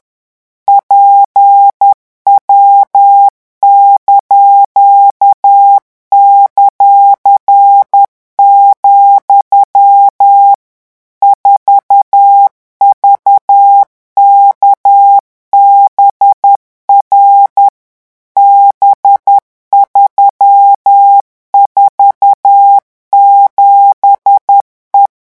Morse code, heard as a keyed tone, about 10 wpm, 800 Hz. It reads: PW);, 4VKBR B347E
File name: morse mp3 / revision17-vitesse 10 mots